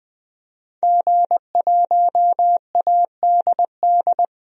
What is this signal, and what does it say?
Morse code, heard as a keyed tone, about 20 wpm, 700 Hz. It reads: G1ADD
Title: DXCC-100 Call Signs - 5 Characters - 20wpm